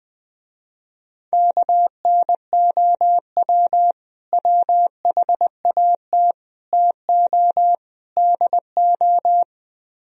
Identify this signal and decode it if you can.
Morse code, heard as a keyed tone, about 20 wpm, 700 Hz. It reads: KNOW WHAT TO DO
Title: Speed-Racing - Sets of 4 Words 17wpm